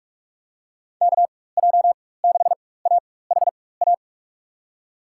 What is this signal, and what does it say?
Morse code, heard as a keyed tone, about 45 wpm, spaced as for 21 wpm, 700 Hz. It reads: KJ6AHA
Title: USA Call Signs - 6 Characters - 4x Character Spacing - 45wpm